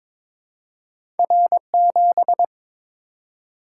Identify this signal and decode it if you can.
Morse code, heard as a keyed tone, about 22 wpm, 700 Hz. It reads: R7
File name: practice / 22wpm / Two Letter-Number Combinations - 22wpm